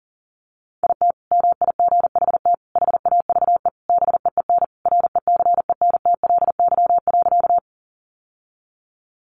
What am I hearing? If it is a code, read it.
Morse code, heard as a keyed tone, about 40 wpm, 700 Hz. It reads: IT MIGHT HAVE BEEN RECENTLY.
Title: Sentences from Top 1200 Words 40wpm